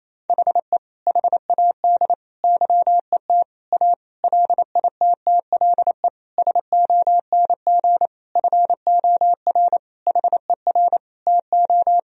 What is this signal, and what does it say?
Morse code, heard as a keyed tone, about 28 wpm, 700 Hz. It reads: HE HAD YET A LITTLE SONG FOR HER TO
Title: Sentences from Top 1000 Words - Review Part 2 - 28wpm